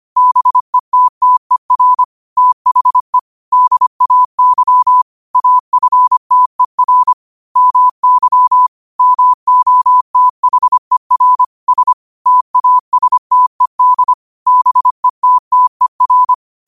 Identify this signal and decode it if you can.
Morse code, heard as a keyed tone, about 25 wpm, 1000 Hz. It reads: BETTER THE DAY AFTER MY MOTHER S TASTED BETTER
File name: 25wpm